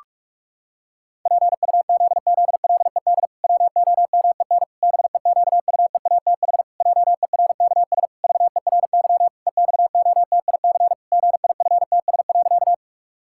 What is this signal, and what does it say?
Morse code, heard as a keyed tone, about 41 wpm, 700 Hz. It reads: PUZZLED WOMEN BEQUEATH JERKS VERY EXOTIC GIFTS.
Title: Pangram Challenge v2 - 45wpm